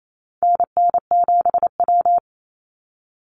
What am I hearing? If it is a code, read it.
Morse code, heard as a keyed tone, about 28 wpm, 700 Hz. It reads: NN7W
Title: Call Signs Repeated 3x - 28wpm